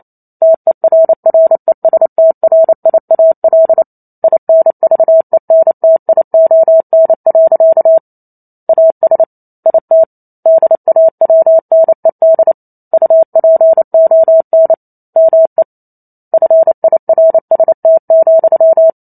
Morse code, heard as a keyed tone, about 28 wpm, 650 Hz. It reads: TERRESTRIAL INVENTION. AS IT DAWNED UPON ME FIRST,